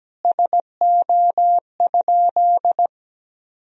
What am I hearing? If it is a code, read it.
Morse code, heard as a keyed tone, about 17 wpm, 700 Hz. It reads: SO?
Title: Sentences from Top 300 Words and Abbreviations 17wpm